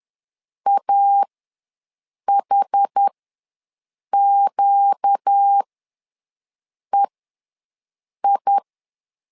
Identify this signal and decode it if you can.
Morse code, heard as a keyed tone, about 10 wpm, spaced as for 6 wpm, 800 Hz. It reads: AHQEI
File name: LEZIONI / 4 Q ws 10wpm 2306